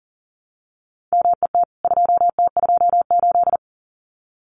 Morse code, heard as a keyed tone, about 40 wpm, 700 Hz. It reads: MET 2T28
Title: CWT Exchange 40wpm